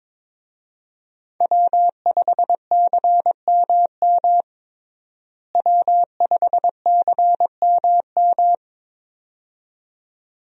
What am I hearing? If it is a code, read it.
Morse code, heard as a keyed tone, about 22 wpm, 700 Hz. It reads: W5CMM W5CMM
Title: USA Call Signs - 5 Characters - Repeated 2x - 22wpm